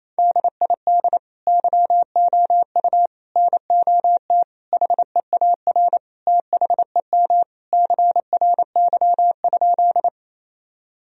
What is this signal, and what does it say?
Morse code, heard as a keyed tone, about 28 wpm, 700 Hz. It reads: DID YOU NOT HEAR THEM CRY?
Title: Sentences from Top 1000 Words - Review Part 2 - 28wpm